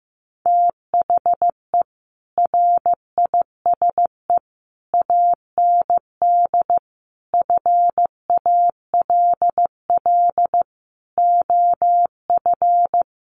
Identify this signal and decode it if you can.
Morse code, heard as a keyed tone, about 15 wpm, 700 Hz. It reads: THE RISE AND FALL OF
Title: Sets of 5 Words Encore2 15wpm